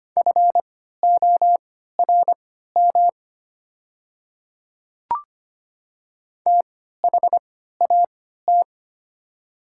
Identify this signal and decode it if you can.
Morse code, heard as a keyed tone, about 25 wpm, spaced as for 14 wpm, 700 Hz. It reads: FORM THAT
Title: 3x Character Spacing - Top 300 Words - 25wpm